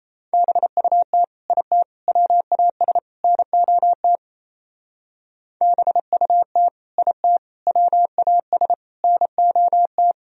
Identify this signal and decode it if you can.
Morse code, heard as a keyed tone, about 32 wpm, 700 Hz. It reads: BUT IT WAS NOT BUT IT WAS NOT
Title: Speed-Racing - Sets of 4 Words 28wpm